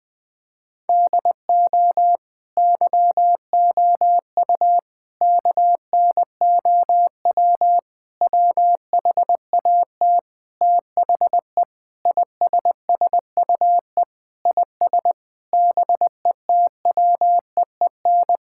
Morse code, heard as a keyed tone, about 21 wpm, 700 Hz. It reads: DO YOU KNOW WHAT THE ISSUE IS BETWEEN
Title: Sentences from Top 1000 Words - Review Part 6 - 20wpm